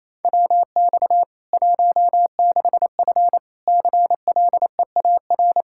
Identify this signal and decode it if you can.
Morse code, heard as a keyed tone, about 27 wpm, 700 Hz. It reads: WX 16F CLEAR
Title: Practice QSO - Exchange 2 - Info - 28wpm